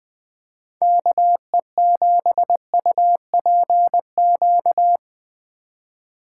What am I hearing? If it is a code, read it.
Morse code, heard as a keyed tone, about 22 wpm, 700 Hz. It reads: KE7UPQ
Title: Speed Racing - USA 2x3 Call Signs 20wpm